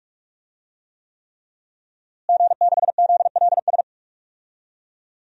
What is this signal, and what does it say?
Morse code, heard as a keyed tone, about 46 wpm, 700 Hz. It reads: G6ZLS
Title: DXCC-100 Call Signs - 5 Characters - Repeated 3x - 45wpm